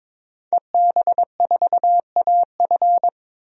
Morse code, heard as a keyed tone, about 22 wpm, 700 Hz. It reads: EB4AF
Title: DXCC-100 Call Signs - 5 Characters - 22wpm